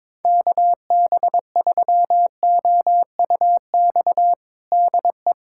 Morse code, heard as a keyed tone, about 21 wpm, 700 Hz. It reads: KB3OUX DE
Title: USA Callsign DE Callsign - 6 Characters - 22wpm